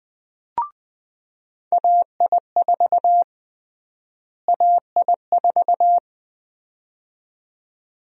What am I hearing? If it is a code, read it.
Morse code, heard as a keyed tone, about 20 wpm, 700 Hz. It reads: AI4 AI4
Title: USA Call Sign Prefixes - Repeated 2x - 20wpm